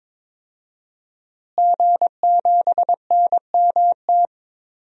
Morse code, heard as a keyed tone, about 22 wpm, 700 Hz. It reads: G7NMT
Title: DXCC-100 Call Signs - 5 Characters - Repeated 3x - 22wpm